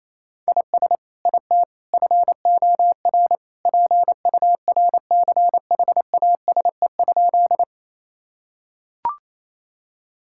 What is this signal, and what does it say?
Morse code, heard as a keyed tone, about 28 wpm, 700 Hz. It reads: IS IT FOR PURCHASE?
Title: Sentences from Top 1000 Words - Review Part 5 - 28wpm